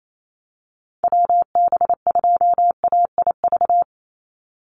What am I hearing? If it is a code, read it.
Morse code, heard as a keyed tone, about 28 wpm, 700 Hz. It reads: WB2AIV
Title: Call Signs Repeated 2x - 28wpm